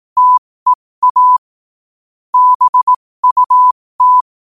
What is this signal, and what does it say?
Morse code, heard as a keyed tone, about 18 wpm, spaced as for 15 wpm, 1000 Hz. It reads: TEA BUT
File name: qotdfiles / 15wpm